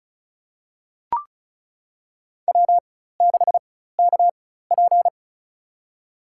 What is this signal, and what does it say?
Morse code, heard as a keyed tone, about 34 wpm, spaced as for 16 wpm, 700 Hz. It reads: W6KP
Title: USA Call Signs - 4 Characters - 4x Character Spacing - 35wpm